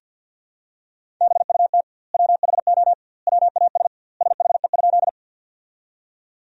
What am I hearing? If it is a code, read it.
Morse code, heard as a keyed tone, about 50 wpm, 700 Hz. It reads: BUT WHO WAS SHE?
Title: 2x Word-Spacing - Sentences from Top 100 Words - 50wpm